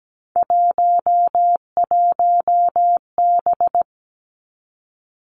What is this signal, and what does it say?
Morse code, heard as a keyed tone, about 17 wpm, 700 Hz. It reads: 11B
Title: Three Challenging Letter-Number Combinations 17wpm